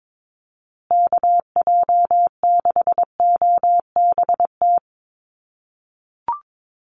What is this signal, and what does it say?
Morse code, heard as a keyed tone, about 22 wpm, 700 Hz. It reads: KJ6OBT
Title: USA Call Signs Repeated 3x 22wpm